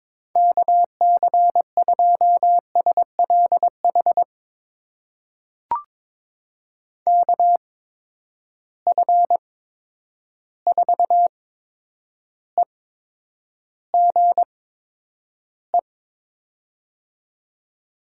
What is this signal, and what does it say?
Morse code, heard as a keyed tone, about 22 wpm, spaced as for 9 wpm, 700 Hz. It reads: KC2SLH KF4EGE
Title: USA Call Signs - 6 Characters - 8x Character Spacing - 22wpm